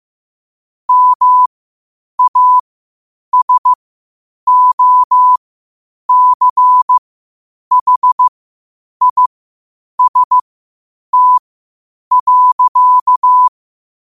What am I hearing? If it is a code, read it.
Morse code, heard as a keyed tone, about 15 wpm, spaced as for 8 wpm, 1000 Hz. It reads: MASOCHIST.
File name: qotdfiles / 8wpm